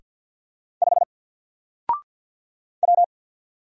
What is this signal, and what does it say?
Morse code, heard as a keyed tone, about 47 wpm, 700 Hz. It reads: 5 W
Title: Single Letter-Number Alt 50wpm